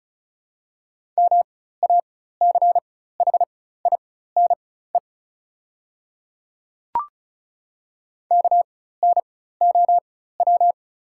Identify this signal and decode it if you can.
Morse code, heard as a keyed tone, about 34 wpm, spaced as for 16 wpm, 690 Hz. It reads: MACHINE KNOW
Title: 4x Character Spacing - Top 500 Words - 35wpm